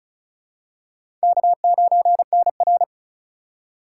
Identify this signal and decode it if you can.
Morse code, heard as a keyed tone, about 36 wpm, 700 Hz. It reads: K9NR
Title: K1USN SST QSO Practice - 35wpm